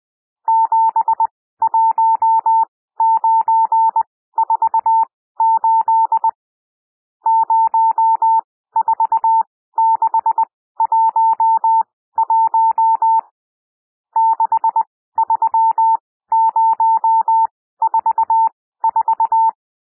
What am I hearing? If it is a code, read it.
Morse code, heard as a keyed tone, about 20 wpm, spaced as for 14 wpm, 900 Hz. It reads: 71948 04611 63044
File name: cwmp3 / N1 SOUND NUMBERS 20 wpm